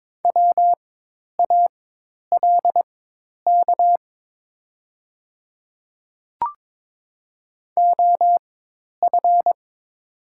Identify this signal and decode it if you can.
Morse code, heard as a keyed tone, about 22 wpm, spaced as for 10 wpm, 700 Hz. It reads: WALK OF
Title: 4x Character Spacing - Top 500 Words - 22wpm